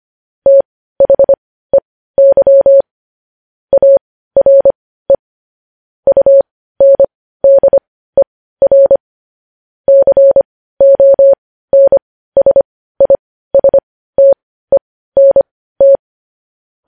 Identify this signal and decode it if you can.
Morse code, heard as a keyed tone, about 25 wpm, spaced as for 15 wpm, 550 Hz. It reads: THEY ARE UNDER CONSISTENT
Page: news headlines in morse code at 15 wpm